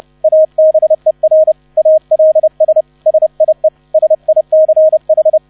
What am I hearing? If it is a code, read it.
Morse code, heard as a keyed tone, about 30 wpm, 630 Hz. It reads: ABER ALS SIE SICH